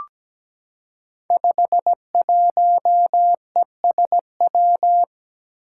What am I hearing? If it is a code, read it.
Morse code, heard as a keyed tone, about 17 wpm, 700 Hz. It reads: 51ESW
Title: Five Challenging Letter-Number Combinations 17wpm